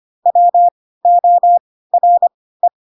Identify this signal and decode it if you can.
Morse code, heard as a keyed tone, about 25 wpm, spaced as for 16 wpm, 700 Hz. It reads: WORE
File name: Pinocchio / Pinocchio ch2 10 25 ik1hge